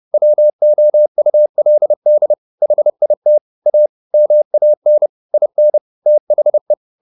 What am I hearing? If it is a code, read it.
Morse code, heard as a keyed tone, about 30 wpm, 600 Hz. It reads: WOULD HIT A MAN IN THE